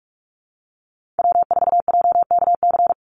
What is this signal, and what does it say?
Morse code, heard as a keyed tone, about 44 wpm, 700 Hz. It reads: W4JKC
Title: NA SOTA Chaser Callsigns 45wpm